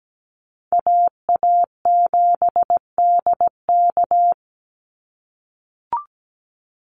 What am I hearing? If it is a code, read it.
Morse code, heard as a keyed tone, about 17 wpm, 700 Hz. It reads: AA7DK
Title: NA SOTA Chaser Callsigns 17wpm